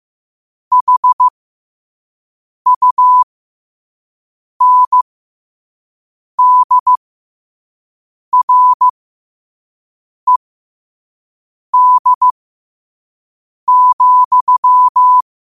Morse code, heard as a keyed tone, about 15 wpm, spaced as for 5 wpm, 1000 Hz. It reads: HUNDRED,
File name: qotdfiles / 5wpm